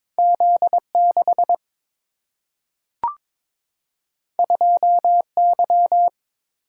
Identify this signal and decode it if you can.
Morse code, heard as a keyed tone, about 22 wpm, 700 Hz. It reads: Z 6 2 Y
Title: Single-Letter Single-Number 22wpm